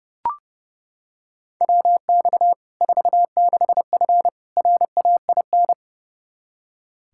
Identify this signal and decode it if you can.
Morse code, heard as a keyed tone, about 30 wpm, 700 Hz. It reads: WX 46F RAIN
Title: Practice QSO - Exchange 2 - Info - 30wpm